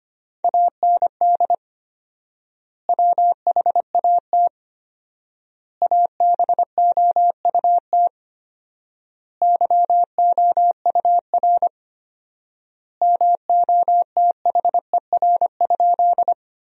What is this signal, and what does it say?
Morse code, heard as a keyed tone, about 25 wpm, 700 Hz. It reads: AND WHAT ABOUT YOUR MOTHER?
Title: 4x Word-Spacing - Sentences from Top 200 Words - 25wpm